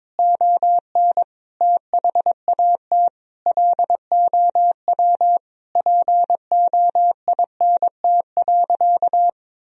Morse code, heard as a keyed tone, about 22 wpm, 700 Hz. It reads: ON THAT LOW POINT.
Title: Sentences from Top 1000 Words - Review Part 1 - 22wpm